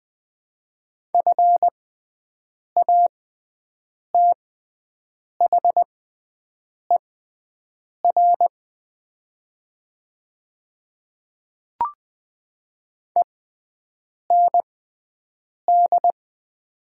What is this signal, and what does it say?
Morse code, heard as a keyed tone, about 20 wpm, spaced as for 7 wpm, 700 Hz. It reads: FATHER END
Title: 6x Character Spacing - Top 300 Words - 20wpm